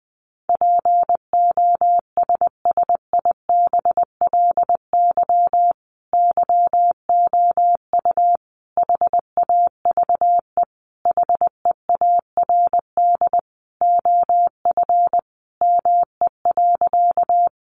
Morse code, heard as a keyed tone, about 20 wpm, 700 Hz. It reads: POSSIBLY YOU HAVE HEARD OF ME.
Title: Sentences from Top 1300 Words 20wpm